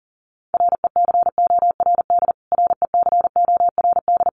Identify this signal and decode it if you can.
Morse code, heard as a keyed tone, about 37 wpm, 700 Hz. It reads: RECORD RECORD
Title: Top 500 Words Repeated 40wpm